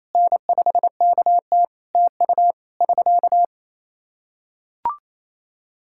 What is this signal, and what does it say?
Morse code, heard as a keyed tone, about 28 wpm, 700 Hz. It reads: N5KT TU <SK>
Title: K1USN SST QSO Practice 28wpm